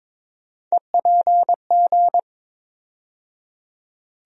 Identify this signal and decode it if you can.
Morse code, heard as a keyed tone, about 22 wpm, 700 Hz. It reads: EPG
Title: USA Call Sign Postfixes - 3 Letters 22wpm